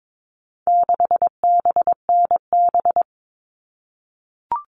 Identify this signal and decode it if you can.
Morse code, heard as a keyed tone, about 22 wpm, 700 Hz. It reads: 6BNB
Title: All Cut Numbers - 4-Digit 22wpm